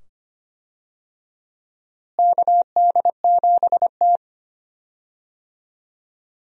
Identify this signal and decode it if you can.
Morse code, heard as a keyed tone, about 25 wpm, 700 Hz. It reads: KD7T
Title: USA Call Signs - 4 Characters - Repeated 2x - 25wpm